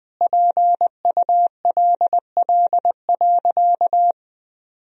Morse code, heard as a keyed tone, about 21 wpm, 700 Hz. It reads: PULL.
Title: Sentences from Top 1000 Words - Review Part 2 - 20wpm